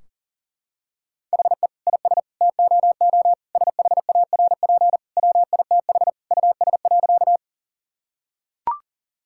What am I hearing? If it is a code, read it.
Morse code, heard as a keyed tone, about 40 wpm, 700 Hz. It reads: HE IS TOO SHARP WITH US.
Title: Sentences from Top 1600 Words 40wpm